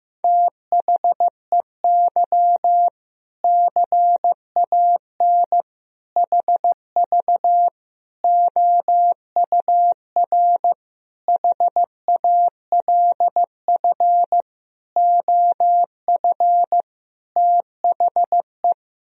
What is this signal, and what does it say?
Morse code, heard as a keyed tone, about 15 wpm, 700 Hz. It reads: THEY CAN HV OUR HALF OF THE
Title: Sentences from Top 400 Words and Abbreviations 15wpm